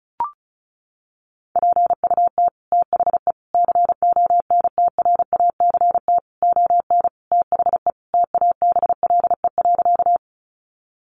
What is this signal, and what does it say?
Morse code, heard as a keyed tone, about 34 wpm, 700 Hz. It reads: PUT THE CONTRACT ON THE TABLE.
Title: Sentences from Top 1400 Words 35wpm